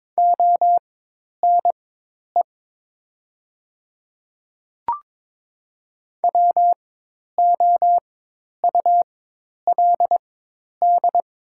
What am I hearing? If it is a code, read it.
Morse code, heard as a keyed tone, about 22 wpm, spaced as for 10 wpm, 700 Hz. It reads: ONE WOULD
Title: 4x Character Spacing - Top 200 Words - 22wpm